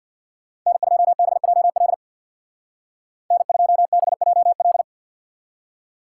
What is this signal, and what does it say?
Morse code, heard as a keyed tone, about 65 wpm, 700 Hz. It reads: N2BJL N2BJL